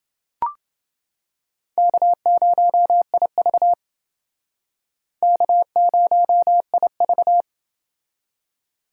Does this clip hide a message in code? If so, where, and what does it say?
Yes: Morse code from 1.8 to 7.4 s, heard as a keyed tone, about 29 wpm, 700 Hz. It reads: K0IV K0IV